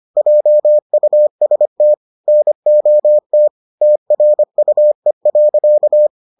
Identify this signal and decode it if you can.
Morse code, heard as a keyed tone, about 25 wpm, 600 Hz. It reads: JUST NOT TRUE.